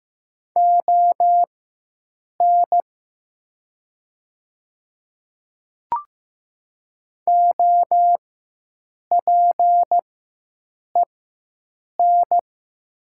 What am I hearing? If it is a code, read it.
Morse code, heard as a keyed tone, about 15 wpm, spaced as for 7 wpm, 700 Hz. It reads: ON OPEN
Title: 4x Character Spacing - Top 500 Words - 15wpm